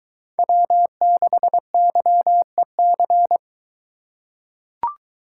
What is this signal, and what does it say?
Morse code, heard as a keyed tone, about 21 wpm, 700 Hz. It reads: W6YEC